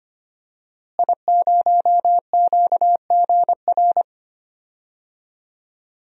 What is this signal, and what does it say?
Morse code, heard as a keyed tone, about 25 wpm, 700 Hz. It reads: I0QGR
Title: DXCC-100 Call Signs - 5 Characters - Repeated 2x - 25wpm